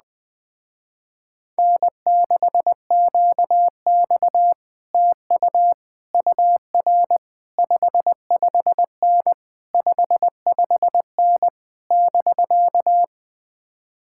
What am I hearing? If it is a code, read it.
Morse code, heard as a keyed tone, about 20 wpm, 700 Hz. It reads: N6QX TU UR 55N 55N <BK>
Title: Speed-Racing Common QSO Phrases - 17wpm